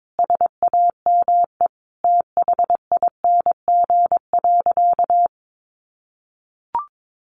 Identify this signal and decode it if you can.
Morse code, heard as a keyed tone, about 22 wpm, 700 Hz. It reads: SAME THING.
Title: Sentences from Top 1400 Words 22wpm